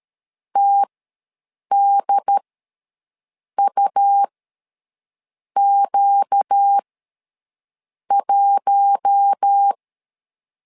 Morse code, heard as a keyed tone, about 13 wpm, spaced as for 6 wpm, 800 Hz. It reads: TDUQ1
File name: LEZIONI / 4 Q1 ws 12wpm 2129